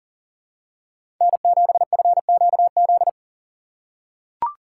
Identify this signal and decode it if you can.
Morse code, heard as a keyed tone, about 39 wpm, 690 Hz. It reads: N7FQZ
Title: USA Call Signs - 4x Character Spacing - 40wpm